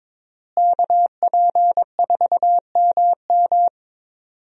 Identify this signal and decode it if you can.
Morse code, heard as a keyed tone, about 22 wpm, 700 Hz. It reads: KP4MM
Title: USA Call Signs - 5 Characters - Repeated 2x - 22wpm